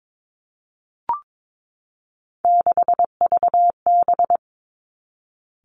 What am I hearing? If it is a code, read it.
Morse code, heard as a keyed tone, about 22 wpm, 700 Hz. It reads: 6VB
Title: Three Challenging Letter-Number Combinations 22wpm